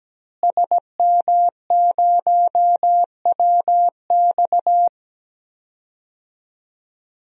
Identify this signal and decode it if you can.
Morse code, heard as a keyed tone, about 17 wpm, 700 Hz. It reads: SM0WX
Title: DXCC-100 Call Signs - 5 Characters - Repeated 2x - 17wpm